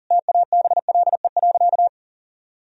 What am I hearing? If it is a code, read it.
Morse code, heard as a keyed tone, about 40 wpm, 700 Hz. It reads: TABLE.
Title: Sentences from Top 1000 Words - Review Part 4 - 40wpm